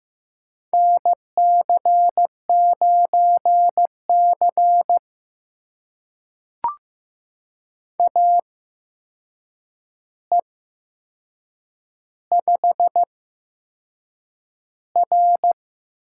Morse code, heard as a keyed tone, about 15 wpm, spaced as for 6 wpm, 700 Hz. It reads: NC9C AE5R
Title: USA Call Signs - 4 Characters - 8x Character Spacing - 15wpm